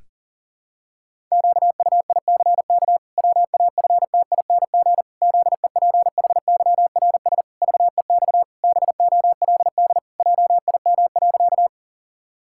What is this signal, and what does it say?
Morse code, heard as a keyed tone, about 40 wpm, 700 Hz. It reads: QUICK WAFTING ZEPHYRS VEX BOLD JIM.
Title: Pangram Challenge v2 - 40wpm